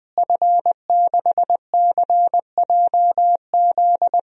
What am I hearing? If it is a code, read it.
Morse code, heard as a keyed tone, about 20 wpm, 700 Hz. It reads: F6CJZ
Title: DXCC-100 Call Signs - 5 Characters - Repeated 3x - 20wpm